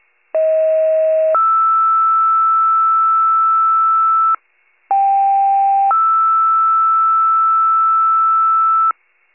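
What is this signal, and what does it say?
Two-tone page, heard as a one-second tone, then a three-second tone, about 640 then 1320 Hz, about 790 then 1320 Hz. It is heard from 0.3 to 8.9 s.
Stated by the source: County-wide two-tone paging alert.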